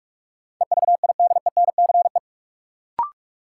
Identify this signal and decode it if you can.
Morse code, heard as a keyed tone, about 46 wpm, 700 Hz. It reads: EVIDENCE
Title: Sentences from Top 1300 Words - New Word Drill Alt - 45wpm